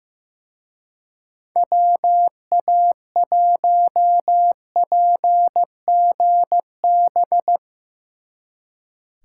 Morse code, heard as a keyed tone, about 16 wpm, 700 Hz. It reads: WA1PGB
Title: Speed Racing - USA 2x3 Call Signs 15wpm